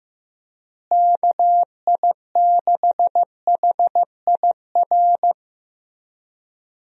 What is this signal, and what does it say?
Morse code, heard as a keyed tone, about 15 wpm, 700 Hz. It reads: KI6HIR
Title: USA Call Signs - 6 Characters - Repeated 3x - 15wpm